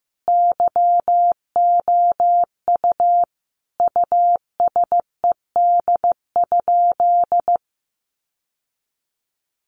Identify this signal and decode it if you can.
Morse code, heard as a keyed tone, about 14 wpm, 700 Hz. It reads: YOU USED?
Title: Sentences from Top 1200 Words 15wpm